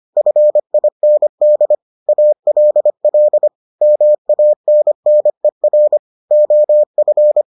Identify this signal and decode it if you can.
Morse code, heard as a keyed tone, about 25 wpm, 600 Hz. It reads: FIND ALL MANNER OF